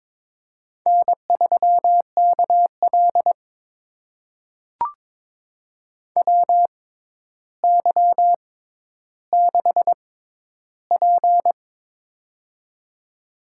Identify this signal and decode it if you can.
Morse code, heard as a keyed tone, about 22 wpm, spaced as for 11 wpm, 700 Hz. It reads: N3KL WY6P
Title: USA Call Signs - 4 Characters - 6x Character Spacing - 22wpm